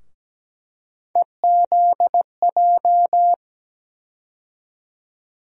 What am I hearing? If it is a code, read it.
Morse code, heard as a keyed tone, about 17 wpm, 700 Hz. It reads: EZJ
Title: USA Call Sign Postfixes - 3 Letters 17wpm